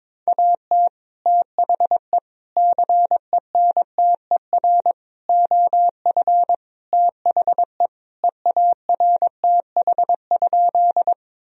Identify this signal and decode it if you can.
Morse code, heard as a keyed tone, about 22 wpm, 700 Hz. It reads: AT THE CENTER OF THE EARTH?
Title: Sentences from Top 1000 Words - Review Part 2 - 22wpm